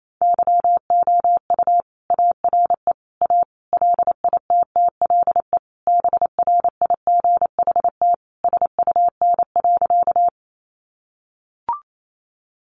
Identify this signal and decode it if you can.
Morse code, heard as a keyed tone, about 28 wpm, 700 Hz. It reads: YOU ARE A LITTLE BRIGHT SUN.
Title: Sentences from Top 1000 Words 28wpm